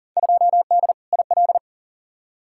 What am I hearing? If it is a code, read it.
Morse code, heard as a keyed tone, about 37 wpm, 700 Hz. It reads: 2D IL
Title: Field Day Exchanges - 40wpm